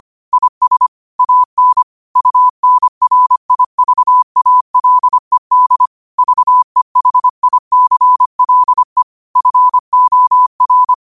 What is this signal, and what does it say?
Morse code, heard as a keyed tone, about 25 wpm, 1000 Hz. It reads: IS AN UNRIVALED VEHICLE FOR